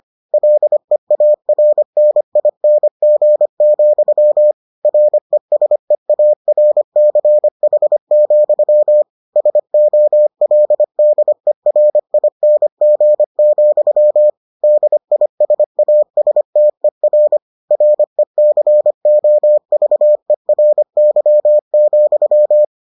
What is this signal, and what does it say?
Morse code, heard as a keyed tone, about 25 wpm, 600 Hz. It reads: LEARNING, RESEARCH, SOLDERING, DISASTER RECOVERY,